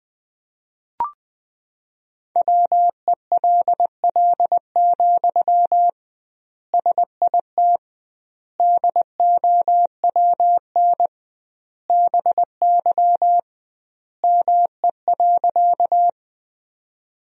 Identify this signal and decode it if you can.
Morse code, heard as a keyed tone, about 21 wpm, 700 Hz. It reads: WELL, SIT DOWN BY ME.
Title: 2x Word-Spacing - Sentences from Top 500 Words - 20wpm